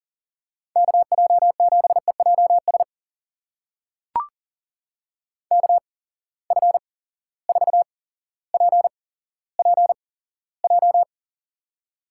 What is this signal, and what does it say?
Morse code, heard as a keyed tone, about 40 wpm, spaced as for 17 wpm, 700 Hz. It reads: KJ7EJS KF4PPJ
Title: USA Call Signs - 6 Characters - 8x Character Spacing - 40wpm